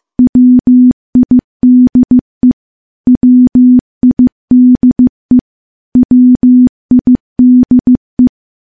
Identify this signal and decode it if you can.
Morse code, heard as a keyed tone, about 15 wpm, 250 Hz. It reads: WIDE WIDE WIDE
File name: morse-training-15WPM